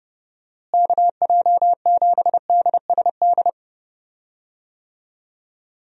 Morse code, heard as a keyed tone, about 30 wpm, 700 Hz. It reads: KJ7DSD
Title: USA Call Signs - 6 Characters - Repeated 3x - 30wpm